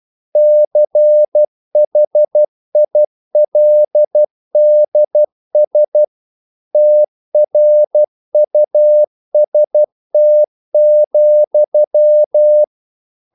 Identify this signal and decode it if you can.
Morse code, heard as a keyed tone, about 12 wpm, 600 Hz. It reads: CHILDS TRUST,
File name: mp3 files / the gardener - 12wpm